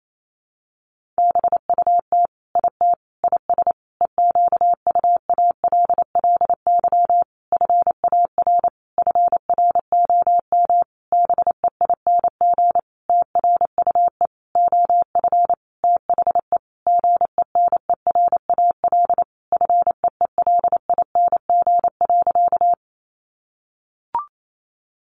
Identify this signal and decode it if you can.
Morse code, heard as a keyed tone, about 28 wpm, 700 Hz. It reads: BUT IT IS EQUALLY FAR FROM BEING TRUE OF THE GENERAL FEELING.
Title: Sentences from Top 1400 Words 28wpm